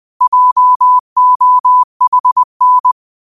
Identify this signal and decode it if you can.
Morse code, heard as a keyed tone, about 20 wpm, 1000 Hz. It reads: JOHN